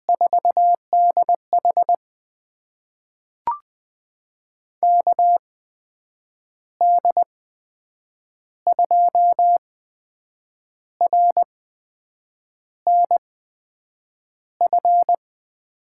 Morse code, heard as a keyed tone, about 20 wpm, spaced as for 7 wpm, 700 Hz. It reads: 4DH KD2RNF
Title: USA Call Signs - 8x Character Spacing - 20wpm